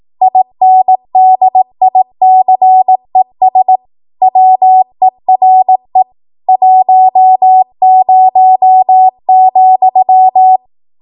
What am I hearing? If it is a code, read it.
Morse code, heard as a keyed tone, about 18 wpm, 750 Hz. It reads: INDICES WERE 10,
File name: Archive / 18 WPM / 18WPM